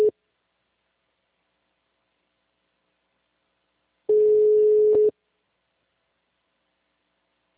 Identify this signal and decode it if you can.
Ringback tone (Europe, 425 Hz).